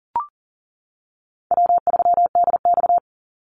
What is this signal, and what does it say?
Morse code, heard as a keyed tone, about 40 wpm, 700 Hz. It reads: W3DX
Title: NA SOTA Activator Callsigns 40wpm